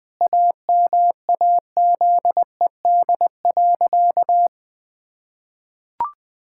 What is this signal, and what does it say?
Morse code, heard as a keyed tone, about 20 wpm, 700 Hz. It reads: AMAZED.
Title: Sentences with 4-Word Phrases - 20wpm